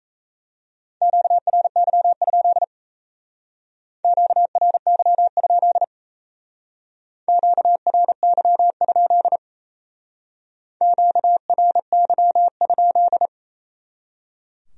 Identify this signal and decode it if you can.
Morse code, heard as a keyed tone, about 36 wpm, 700 Hz. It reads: QRY? QRY? QRY? QRY?